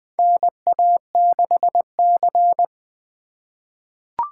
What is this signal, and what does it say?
Morse code, heard as a keyed tone, about 20 wpm, 700 Hz. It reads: NA6C
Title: USA Call Signs - 4 Characters - 3x Character Spacing - 20wpm